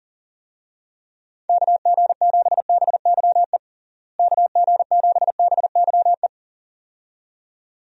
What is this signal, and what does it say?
Morse code, heard as a keyed tone, about 40 wpm, 700 Hz. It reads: KG7BYE KG7BYE
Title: USA Call Signs - 6 Characters - Repeated 2x - 40wpm